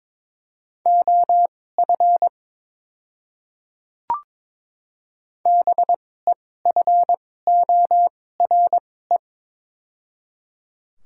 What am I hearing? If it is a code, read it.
Morse code, heard as a keyed tone, about 22 wpm, spaced as for 16 wpm, 700 Hz. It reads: OF BEFORE
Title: 2x Character Spacing - Top 200 Words - 22wpm